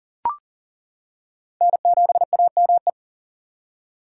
Morse code, heard as a keyed tone, about 40 wpm, 690 Hz. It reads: N7AME
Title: K1USN SST QSO Practice - 40wpm